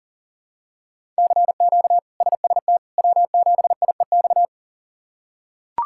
Morse code, heard as a keyed tone, about 40 wpm, 700 Hz. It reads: CQ SST W7IEX
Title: K1USN SST QSO Practice - 40wpm